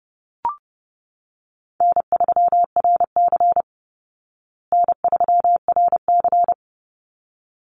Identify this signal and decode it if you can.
Morse code, heard as a keyed tone, about 30 wpm, 700 Hz. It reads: N3RC N3RC
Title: Call Signs Repeated 3x - 30wpm